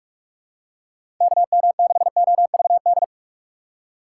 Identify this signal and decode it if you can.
Morse code, heard as a keyed tone, about 41 wpm, 700 Hz. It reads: KM6OVD
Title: USA Call Signs - 6 Characters - 2x Character Spacing - 45wpm